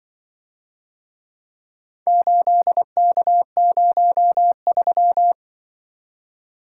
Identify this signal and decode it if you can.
Morse code, heard as a keyed tone, about 25 wpm, 700 Hz. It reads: 8K03
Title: Speed-Racing - Four Letter-Number Combinations - 20wpm